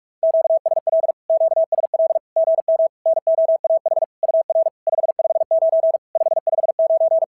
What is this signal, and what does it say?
Morse code, heard as a keyed tone, about 44 wpm, 660 Hz. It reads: QSL QSL GM NOAH UR 559 559
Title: SOTA QSO Practice 45wpm